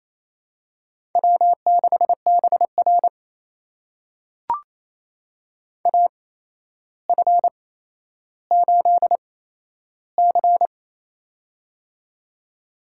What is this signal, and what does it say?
Morse code, heard as a keyed tone, about 27 wpm, spaced as for 12 wpm, 700 Hz. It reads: W6BR AF8C
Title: USA Call Signs - 4 Characters - 8x Character Spacing - 28wpm